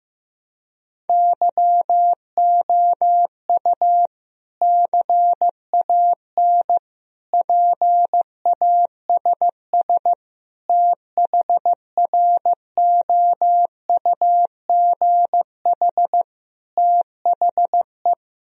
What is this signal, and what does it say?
Morse code, heard as a keyed tone, about 15 wpm, 700 Hz. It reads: YOU CAN PASS THROUGH THE
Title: Sentences from Top 1000 Words - Review Part 1 - 15wpm